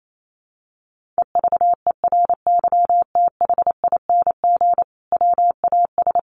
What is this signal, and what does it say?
Morse code, heard as a keyed tone, about 28 wpm, 700 Hz. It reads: EVERYTHING WAS
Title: Sentences from Top 1100 Words 28wpm